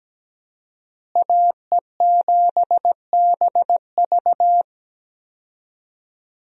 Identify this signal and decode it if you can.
Morse code, heard as a keyed tone, about 17 wpm, 700 Hz. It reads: AE7BV
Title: USA Call Signs - 5 Characters - Repeated 3x - 17wpm